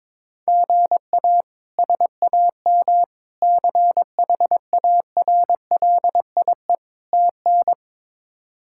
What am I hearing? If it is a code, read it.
Morse code, heard as a keyed tone, about 22 wpm, 700 Hz. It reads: GA SAM CHARLIE TN
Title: K1USN SST QSO Practice - 22wpm